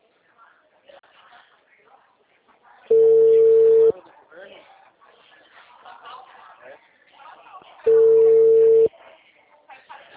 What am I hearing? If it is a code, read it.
Ringback tone (China, 450 Hz), with speech.